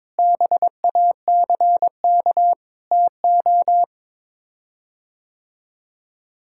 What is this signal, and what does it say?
Morse code, heard as a keyed tone, about 22 wpm, 700 Hz. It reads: BACK TO
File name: practice / 22wpm / Sets of 8 Words - 22wpm